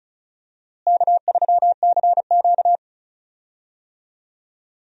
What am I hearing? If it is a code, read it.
Morse code, heard as a keyed tone, about 36 wpm, 700 Hz. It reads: K3CQ
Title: USA Call Signs - 4 Characters - Repeated 3x - 35wpm